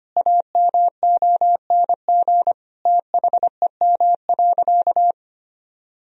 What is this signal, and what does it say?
Morse code, heard as a keyed tone, about 25 wpm, 700 Hz. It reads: AMONG THEM.
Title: Sentences from Top 1000 Words - Review Part 2 - 25wpm